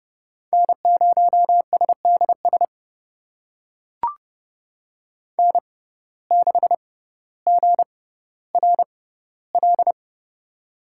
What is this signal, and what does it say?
Morse code, heard as a keyed tone, about 30 wpm, spaced as for 15 wpm, 700 Hz. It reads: N0SDS N6GRL
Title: USA Call Signs - 6x Character Spacing - 30wpm